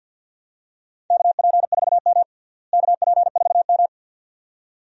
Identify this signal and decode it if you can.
Morse code, heard as a keyed tone, about 47 wpm, 700 Hz. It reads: KP4M KP4M
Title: DXCC-100 Call Signs - 4 Characters - Repeated 2x - 50wpm